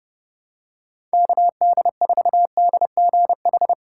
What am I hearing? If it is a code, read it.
Morse code, heard as a keyed tone, about 30 wpm, 690 Hz. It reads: KD4DGH
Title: USA Call Signs - 3x Character Spacing - 30wpm